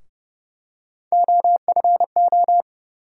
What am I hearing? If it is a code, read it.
Morse code, heard as a keyed tone, about 30 wpm, 700 Hz. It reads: OFO
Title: USA Call Sign Postfixes - 3 Letters 30wpm